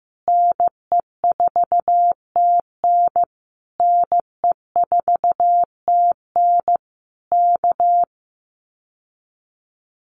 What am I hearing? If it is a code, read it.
Morse code, heard as a keyed tone, about 15 wpm, 700 Hz. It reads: NE4TN NE4TN K
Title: Common QSO Phrases Encore 15wpm